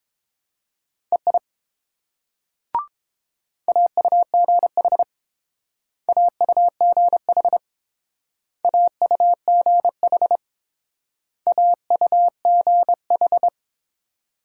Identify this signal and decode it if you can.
Morse code, heard as a keyed tone, about 28 wpm, 700 Hz. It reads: EI AUGH AUGH AUGH AUGH